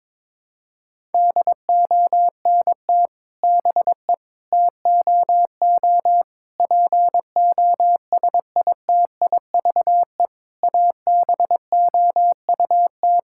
Morse code, heard as a keyed tone, about 22 wpm, 700 Hz. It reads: DONT BE TOO POSITIVE ABOUT
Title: Sentences from Top 1000 Words - Review Part 5 - 22wpm